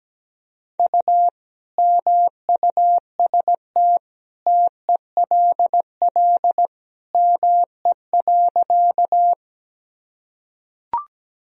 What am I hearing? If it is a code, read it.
Morse code, heard as a keyed tone, about 17 wpm, 700 Hz. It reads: U MUST TELL ME.
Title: Sentences from Top 200 Words and Abbreviations 17wpm